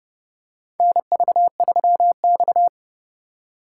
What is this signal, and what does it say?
Morse code, heard as a keyed tone, about 31 wpm, 700 Hz. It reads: NV3X
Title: USA Call Signs - 4 Characters - 4x Character Spacing - 30wpm